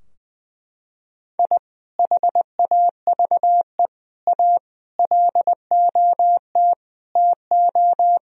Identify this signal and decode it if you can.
Morse code, heard as a keyed tone, about 20 wpm, 700 Hz. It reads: I HAVE A LOT TO
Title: Sentences from Top 1000 Words - Review Part 3 - 20wpm